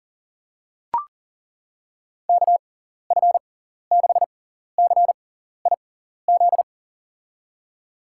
Morse code, heard as a keyed tone, about 40 wpm, spaced as for 14 wpm, 700 Hz. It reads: KF6CIZ
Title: USA Call Signs - 6 Characters - 6x Character Spacing - 40wpm